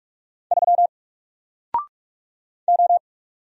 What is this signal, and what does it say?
Morse code, heard as a keyed tone, about 46 wpm, 700 Hz. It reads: 3 O
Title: Single Letter-Number Alt 45wpm